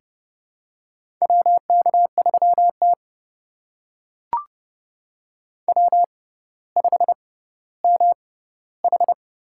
Morse code, heard as a keyed tone, about 30 wpm, spaced as for 15 wpm, 700 Hz. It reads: WK3T W5MH
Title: USA Call Signs - 4 Characters - 6x Character Spacing - 30wpm